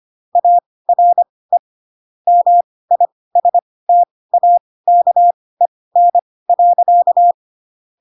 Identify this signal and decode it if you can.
Morse code, heard as a keyed tone, about 25 wpm, spaced as for 18 wpm, 700 Hz. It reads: ARE MISTAKEN.
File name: Pinocchio / Pinocchio ch1 12 25 ik1hge